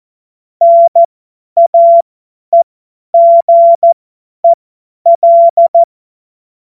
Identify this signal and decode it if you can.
Morse code, heard as a keyed tone, about 14 wpm, spaced as for 10 wpm, 680 Hz. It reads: NAEGEL
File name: Morsekurs / Witze / Witz 04. Januar 2022, 50 F70